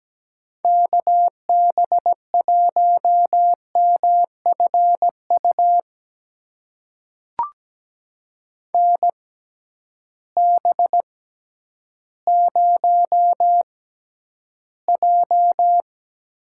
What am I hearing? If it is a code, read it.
Morse code, heard as a keyed tone, about 17 wpm, spaced as for 10 wpm, 700 Hz. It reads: KB1MFU NB0J
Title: USA Call Signs - 6x Character Spacing - 17wpm